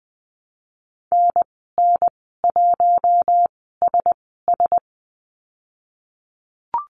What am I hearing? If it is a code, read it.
Morse code, heard as a keyed tone, about 20 wpm, spaced as for 15 wpm, 700 Hz. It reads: NN1SS
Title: 048 Lesson 9g - Number 4 - Call Signs 20wpm